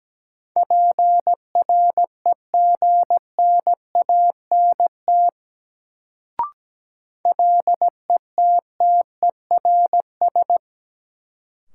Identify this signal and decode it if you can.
Morse code, heard as a keyed tone, about 17 wpm, 700 Hz. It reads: PREGNANT LETTERS
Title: Sentences from Top 1000 Words - Review Part 6 - New Word Drill - 17wpm